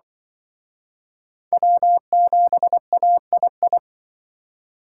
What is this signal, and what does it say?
Morse code, heard as a keyed tone, about 23 wpm, 700 Hz. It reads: W7AII
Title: Speed Racing - USA Call Signs 20wpm